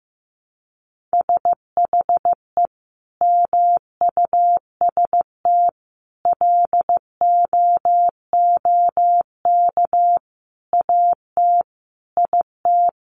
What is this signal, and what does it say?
Morse code, heard as a keyed tone, about 15 wpm, 700 Hz. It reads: SHE MUST LOOK AT IT
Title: Sentences from Top 1400 Words 15wpm